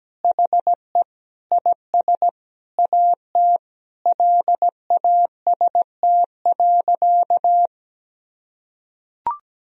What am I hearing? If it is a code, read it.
Morse code, heard as a keyed tone, about 17 wpm, 700 Hz. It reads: HE IS AT LAST.
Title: Sentences from Top 300 Words and Abbreviations 17wpm